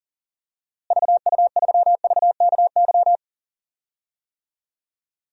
Morse code, heard as a keyed tone, about 37 wpm, 700 Hz. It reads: VU3VKY
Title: DXCC-100 Call Signs - 6 Characters - Repeated 3x - 40wpm